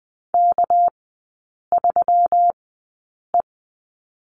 Morse code, heard as a keyed tone, about 20 wpm, spaced as for 8 wpm, 700 Hz. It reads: K3E
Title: 198 Lesson 32e - Letter k - Three Characters 20wpm